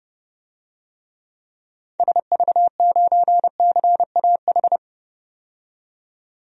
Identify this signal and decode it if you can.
Morse code, heard as a keyed tone, about 32 wpm, 700 Hz. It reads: SV9CAH
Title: DXCC-100 Call Signs - 6 Characters - 30wpm